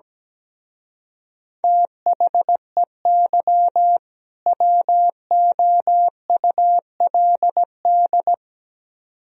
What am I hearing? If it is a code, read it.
Morse code, heard as a keyed tone, about 17 wpm, 700 Hz. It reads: THEY WOULD
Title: Speed-Racing - Sets of 2 Words 17wpm